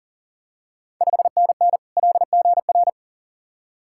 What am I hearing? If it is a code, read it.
Morse code, heard as a keyed tone, about 42 wpm, 690 Hz. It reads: 5NN LGR
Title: Ontario QSO Party Exchange 40wpm